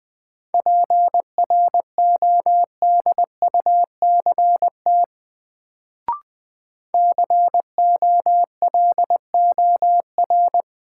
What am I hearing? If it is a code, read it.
Morse code, heard as a keyed tone, about 20 wpm, 700 Hz. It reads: PRODUCT COLOR
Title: Sentences from Top 1000 Words - Review Part 2 - New Word Drill - 20wpm